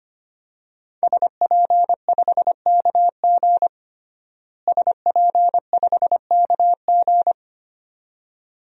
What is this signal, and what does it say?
Morse code, heard as a keyed tone, about 25 wpm, 700 Hz. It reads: SP5KG SP5KG
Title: DXCC-100 Call Signs - 5 Characters - Repeated 2x - 25wpm